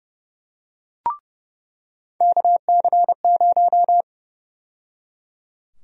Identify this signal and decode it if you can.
Morse code, heard as a keyed tone, about 30 wpm, 700 Hz. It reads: KC0
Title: USA Call Sign Prefixes - 30wpm